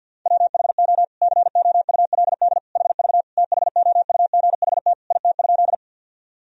Morse code, heard as a keyed tone, about 52 wpm, 700 Hz. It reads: WHO COULD HV THOUGHT IT?
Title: Sentences from Top 300 Words and Abbreviations 50wpm